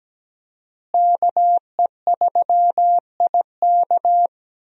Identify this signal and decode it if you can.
Morse code, heard as a keyed tone, about 17 wpm, 700 Hz. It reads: KE3IK
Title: USA Call Signs - 5 Characters - Repeated 3x - 17wpm